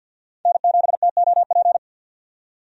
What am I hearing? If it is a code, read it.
Morse code, heard as a keyed tone, about 50 wpm, 700 Hz. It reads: N7TOP
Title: USA Call Signs - 5 Characters - 50wpm